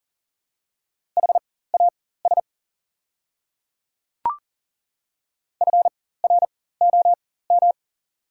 Morse code, heard as a keyed tone, about 40 wpm, spaced as for 19 wpm, 700 Hz. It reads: HAS FROM
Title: 4x Character Spacing - Top 100 Words - 40wpm